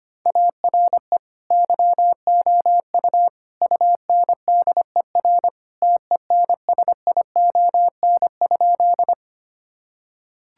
Morse code, heard as a keyed tone, about 25 wpm, 700 Hz. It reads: ARE YOU UNDER TENSION?
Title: Sentences from Top 1600 Words 25wpm